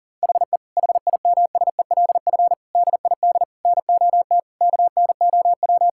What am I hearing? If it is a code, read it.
Morse code, heard as a keyed tone, about 37 wpm, 700 Hz. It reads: HE HIMSELF DID NOT KNOW
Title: Sentences from Top 1600 Words 40wpm